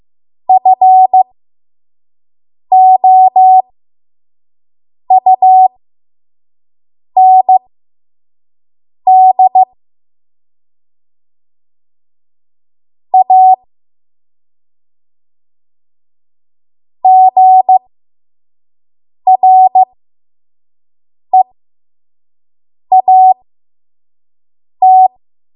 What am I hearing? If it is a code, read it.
Morse code, heard as a keyed tone, about 15 wpm, spaced as for 5 wpm, 750 Hz. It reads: FOUND A GREAT
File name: Archive / 5 WPM / 05WPM